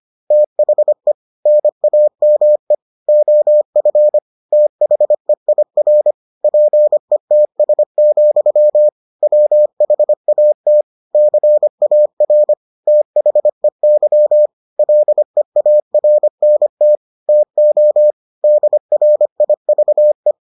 Morse code, heard as a keyed tone, about 25 wpm, 600 Hz. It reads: THE NAME OF THEIR PETS, WHAT CAR THEY LEARNT TO DRIVE